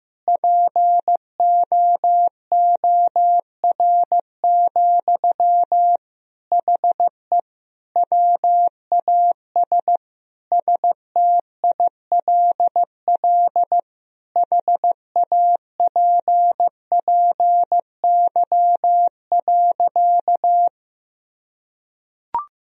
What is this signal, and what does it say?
Morse code, heard as a keyed tone, about 15 wpm, 700 Hz. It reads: POOR, HE WAS STILL HAPPY.
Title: Sentences from Top 1000 Words - Review Part 3 - 15wpm